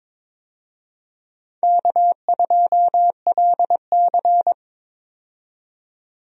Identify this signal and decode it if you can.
Morse code, heard as a keyed tone, about 22 wpm, 700 Hz. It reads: K2LC
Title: USA Call Signs - 4 Characters - 22wpm